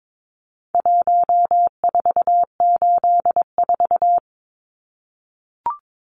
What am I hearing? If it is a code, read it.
Morse code, heard as a keyed tone, about 22 wpm, 700 Hz. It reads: 1484
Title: Numbers - Contest Style - 4-Digit 22wpm